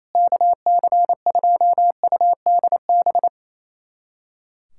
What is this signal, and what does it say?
Morse code, heard as a keyed tone, about 28 wpm, 700 Hz. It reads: KC2UDB
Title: USA Callsign DE Callsign - 6 Characters - 28wpm